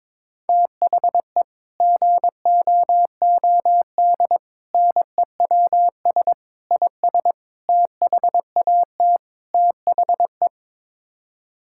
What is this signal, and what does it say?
Morse code, heard as a keyed tone, about 21 wpm, 700 Hz. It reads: THE GOOD NEWS IS THAT THE
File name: practice / 22wpm / Sets of 6 Words - 22wpm